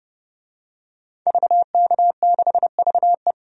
Morse code, heard as a keyed tone, about 30 wpm, 700 Hz. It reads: VK6VE
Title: DXCC-100 Call Signs - 5 Characters - 30wpm